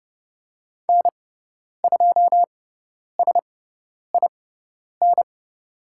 Morse code, heard as a keyed tone, about 30 wpm, spaced as for 10 wpm, 700 Hz. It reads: N2SIN
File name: Lesson 17g - Letter u - Call Signs 30wpm